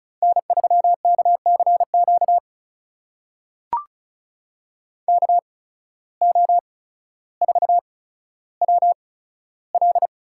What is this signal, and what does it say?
Morse code, heard as a keyed tone, about 34 wpm, spaced as for 15 wpm, 700 Hz. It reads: N3KCQ KO4WL
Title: USA Call Signs - 5 Characters - 8x Character Spacing - 35wpm